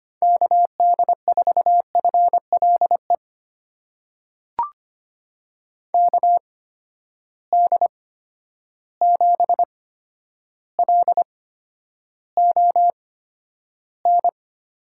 Morse code, heard as a keyed tone, about 25 wpm, spaced as for 11 wpm, 700 Hz. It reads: KD4FLE KD7LON
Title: USA Call Signs - 6 Characters - 8x Character Spacing - 25wpm